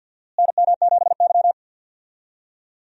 Morse code, heard as a keyed tone, about 49 wpm, 700 Hz. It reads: NM7Y
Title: USA Call Signs - 4 Characters - Repeated 2x - 50wpm